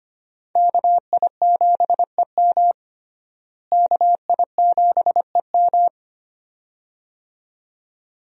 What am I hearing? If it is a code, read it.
Morse code, heard as a keyed tone, about 25 wpm, 700 Hz. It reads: KI7EM KI7EM
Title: USA Call Signs - 5 Characters - Repeated 2x - 25wpm